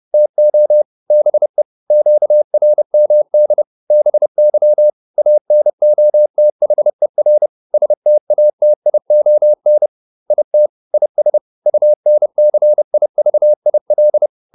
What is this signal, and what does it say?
Morse code, heard as a keyed tone, about 30 wpm, 600 Hz. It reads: TO BE QRMD BY ANOTHER STATION IT IS UNCIVIL